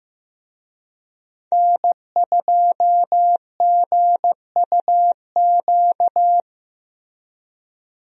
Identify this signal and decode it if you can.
Morse code, heard as a keyed tone, about 15 wpm, 700 Hz. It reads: N2GUQ
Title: USA Call Signs - 5 Characters - Repeated 2x - 15wpm